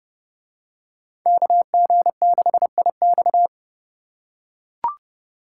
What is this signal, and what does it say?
Morse code, heard as a keyed tone, about 29 wpm, 700 Hz. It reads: KG6IX